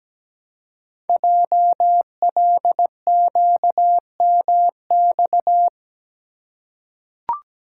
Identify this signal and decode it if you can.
Morse code, heard as a keyed tone, about 17 wpm, 700 Hz. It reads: JLQMX
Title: Code Groups - 5 Letters - 17wpm